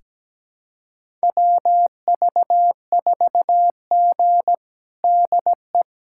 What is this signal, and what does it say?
Morse code, heard as a keyed tone, about 17 wpm, 700 Hz. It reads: WV4G DE
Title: USA Callsign DE Callsign - 4 Characters - 17wpm